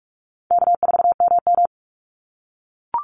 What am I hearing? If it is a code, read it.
Morse code, heard as a keyed tone, about 44 wpm, 700 Hz. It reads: K4MM
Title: Call Signs - 45wpm